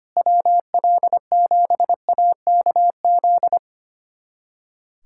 Morse code, heard as a keyed tone, about 25 wpm, 700 Hz. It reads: WL7AKZ
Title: USA Callsign DE Callsign - 6 Characters - 25wpm